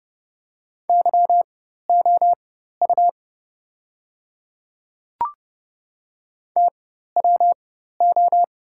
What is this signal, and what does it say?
Morse code, heard as a keyed tone, about 30 wpm, spaced as for 14 wpm, 700 Hz. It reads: YOU TWO
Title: 4x Character Spacing - Top 100 Words - 30wpm